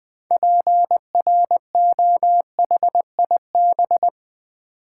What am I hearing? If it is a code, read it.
Morse code, heard as a keyed tone, about 20 wpm, 700 Hz. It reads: PROHIB
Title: Build A Word - 3 Letters At A Time - 8-Letter Words - 20wpm